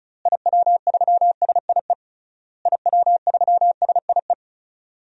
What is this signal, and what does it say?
Morse code, heard as a keyed tone, about 36 wpm, 700 Hz. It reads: IW3SIE IW3SIE
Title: DXCC-100 Call Signs - 6 Characters - Repeated 3x - 35wpm